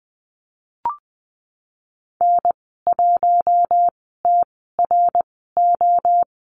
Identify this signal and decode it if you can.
Morse code, heard as a keyed tone, about 20 wpm, spaced as for 15 wpm, 700 Hz. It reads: N1TRO
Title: 076 Lesson 13g - Letter L - Call Signs 20wpm